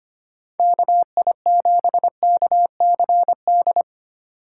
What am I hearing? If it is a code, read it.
Morse code, heard as a keyed tone, about 31 wpm, 700 Hz. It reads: KI7KCD
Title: Speed Racing - USA 2x3 Call Signs 25wpm